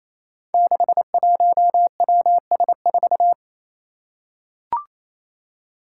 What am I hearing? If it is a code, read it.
Morse code, heard as a keyed tone, about 28 wpm, 700 Hz. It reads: 61WS4
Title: Five Challenging Letter-Number Combinations 28wpm